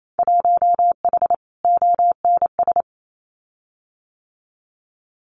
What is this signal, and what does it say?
Morse code, heard as a keyed tone, about 28 wpm, 700 Hz. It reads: 1H ONS
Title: Winter Field Day Exchanges 28wpm